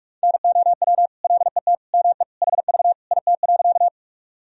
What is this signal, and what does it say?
Morse code, heard as a keyed tone, about 46 wpm, 700 Hz. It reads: NOW LET ME HV IT.
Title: Sentences from Top 300 Words and Abbreviations 45wpm